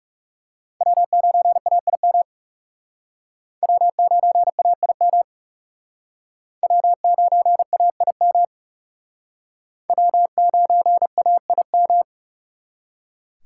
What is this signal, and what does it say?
Morse code, heard as a keyed tone, about 39 wpm, 700 Hz. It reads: W9AIM W9AIM W9AIM W9AIM